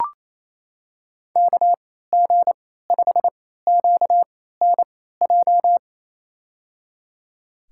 Morse code, heard as a keyed tone, about 28 wpm, spaced as for 16 wpm, 700 Hz. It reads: KG5QNJ
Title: USA Call Signs - 3x Character Spacing - 28wpm